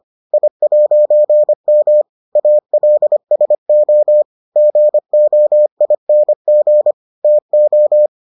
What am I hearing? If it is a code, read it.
Morse code, heard as a keyed tone, about 25 wpm, 600 Hz. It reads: I'M ALSO GOING TO